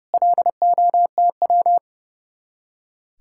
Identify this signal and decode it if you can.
Morse code, heard as a keyed tone, about 37 wpm, 700 Hz. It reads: LOTW
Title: Speed-Racing Common QSO Phrases - 30wpm